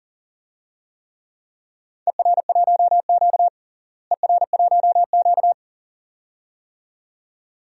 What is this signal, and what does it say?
Morse code, heard as a keyed tone, about 40 wpm, 700 Hz. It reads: ER1Q ER1Q
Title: DXCC-100 Call Signs - 4 Characters - Repeated 2x - 40wpm